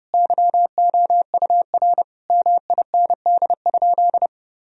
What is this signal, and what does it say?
Morse code, heard as a keyed tone, about 30 wpm, 700 Hz. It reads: YOUR MIND?
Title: Sentences from Top 1000 Words - Review Part 4 - 30wpm